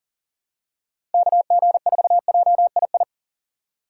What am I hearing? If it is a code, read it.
Morse code, heard as a keyed tone, about 40 wpm, 700 Hz. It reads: KG4JII
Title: USA Call Signs - 6 Characters - 6x Character Spacing - 40wpm